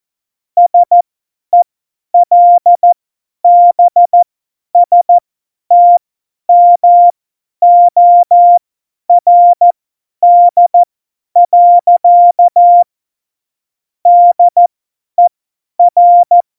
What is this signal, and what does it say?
Morse code, heard as a keyed tone, about 14 wpm, spaced as for 10 wpm, 700 Hz. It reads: SELBSTMORD. DER
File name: Morsekurs / Witze / Witz 17. November 2020, 50 F70